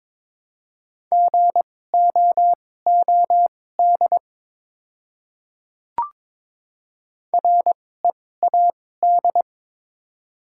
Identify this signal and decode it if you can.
Morse code, heard as a keyed tone, about 22 wpm, spaced as for 16 wpm, 700 Hz. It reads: GOOD READ
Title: 2x Character Spacing - Top 200 Words - 22wpm